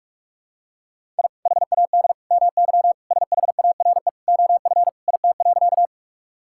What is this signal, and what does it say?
Morse code, heard as a keyed tone, about 46 wpm, 700 Hz. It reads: I HAD MY SHARE OF IT.
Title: Sentences from Top 1000 Words - Review Part 6 - 45wpm